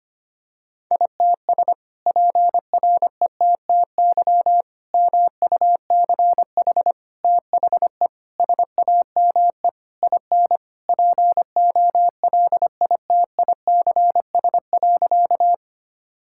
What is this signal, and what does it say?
Morse code, heard as a keyed tone, about 25 wpm, 700 Hz. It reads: ITS PRETTY MUCH THE SAME IN POLITICS.
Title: Sentences from Top 1000 Words - Review Part 6 - 25wpm